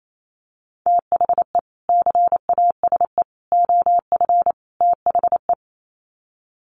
Morse code, heard as a keyed tone, about 28 wpm, 700 Hz. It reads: THE CASE OF THE
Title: Sets of 4 Words 28wpm